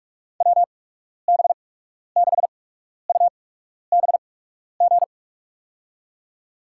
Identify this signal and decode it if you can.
Morse code, heard as a keyed tone, about 44 wpm, spaced as for 12 wpm, 700 Hz. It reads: WB6UBG
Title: USA Call Signs - 6 Characters - 8x Character Spacing - 45wpm